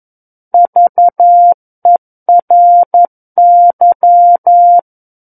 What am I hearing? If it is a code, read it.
Morse code, heard as a keyed tone, about 11 wpm, 700 Hz. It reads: VERY
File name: 11 WPM